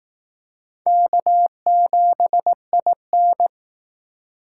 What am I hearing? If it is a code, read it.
Morse code, heard as a keyed tone, about 18 wpm, 700 Hz. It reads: K7IN
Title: Speed Racing - USA 1x2 Call Signs 15wpm